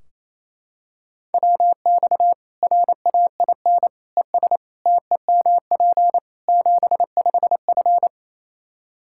Morse code, heard as a keyed tone, about 28 wpm, 700 Hz. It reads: WX RAIN ES TEMP 75F
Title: Practice QSO - Exchange 2 - Info - 28wpm